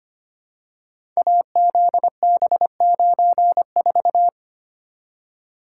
Morse code, heard as a keyed tone, about 25 wpm, 700 Hz. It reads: AZB94
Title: Code Groups - 5 Letter:Numbers - 25wpm